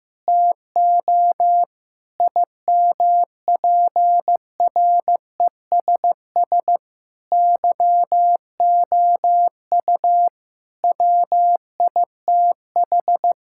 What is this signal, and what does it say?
Morse code, heard as a keyed tone, about 15 wpm, 700 Hz. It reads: TO IMPRESS YOU WITH
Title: Sentences from Top 1600 Words 15wpm